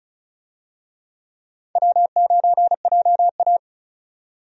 Morse code, heard as a keyed tone, about 34 wpm, 700 Hz. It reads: W9JA
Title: USA Call Signs - 4 Characters - 35wpm